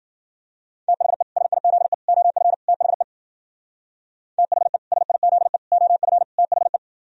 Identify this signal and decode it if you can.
Morse code, heard as a keyed tone, about 49 wpm, 700 Hz. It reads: THE SIZE OF THE THE SIZE OF THE
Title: Speed-Racing - Sets of 4 Words 40wpm